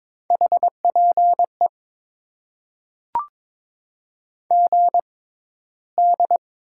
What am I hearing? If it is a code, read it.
Morse code, heard as a keyed tone, about 22 wpm, spaced as for 13 wpm, 700 Hz. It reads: HPE GD
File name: Top 90 QSO Elements - 6x Character Spacing - 22wpm